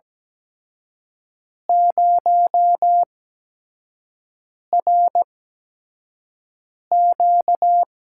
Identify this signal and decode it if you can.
Morse code, heard as a keyed tone, about 17 wpm, spaced as for 5 wpm, 700 Hz. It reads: 0RQ
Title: USA Call Signs - 4 Characters - 8x Character Spacing - 17wpm